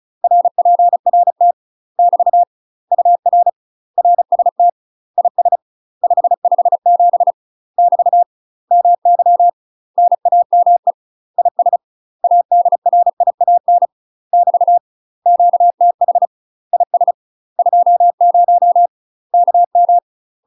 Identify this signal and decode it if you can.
Morse code, heard as a keyed tone, about 34 wpm, 700 Hz. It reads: RPRT <BT> UR RST IS 557 <BT> MY NAME IS ADRIAN <BT> QTH IS 20 KM
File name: UK MASTER QSOs TRAINING / 35WPM / UK MASTER QSOs 0039